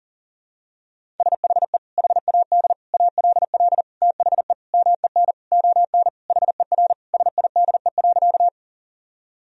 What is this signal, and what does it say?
Morse code, heard as a keyed tone, about 40 wpm, 700 Hz. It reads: SHE HAD ALL THE MEN ON HER SIDE.
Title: Sentences from Top 1000 Words - Review Part 1 - 40wpm